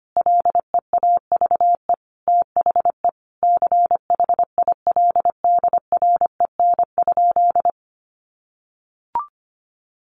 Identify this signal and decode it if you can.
Morse code, heard as a keyed tone, about 25 wpm, 700 Hz. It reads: LEAVE THE CHILDREN?
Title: Sentences from Top 1400 Words 25wpm